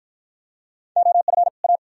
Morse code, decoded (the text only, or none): OF A